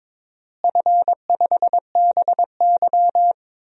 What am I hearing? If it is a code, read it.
Morse code, heard as a keyed tone, about 22 wpm, 700 Hz. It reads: F5BY
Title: DXCC-100 Call Signs - 4 Characters - Repeated 3x - 22wpm